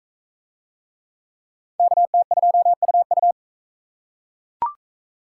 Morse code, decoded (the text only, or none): KT2UU